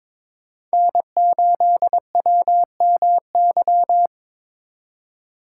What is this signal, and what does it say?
Morse code, heard as a keyed tone, about 25 wpm, 700 Hz. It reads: N8WMY
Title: Speed Racing - USA 1x3 Call Signs 22wpm